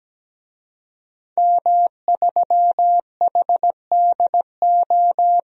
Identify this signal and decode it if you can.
Morse code, heard as a keyed tone, about 17 wpm, 700 Hz. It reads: M3HDO
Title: DXCC-100 Call Signs - 5 Characters - 17wpm